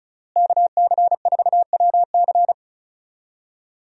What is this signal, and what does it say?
Morse code, heard as a keyed tone, about 36 wpm, 700 Hz. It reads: KC4WC
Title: USA Call Signs - 5 Characters - Repeated 3x - 35wpm